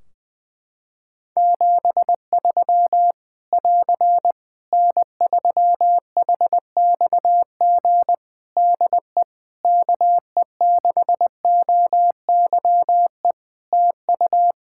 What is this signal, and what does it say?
Morse code, heard as a keyed tone, about 20 wpm, 700 Hz. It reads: 73 <AR> N3HXG DE KE6OYE TU
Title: Practice QSO - Exchange 4 - Ending - 20wpm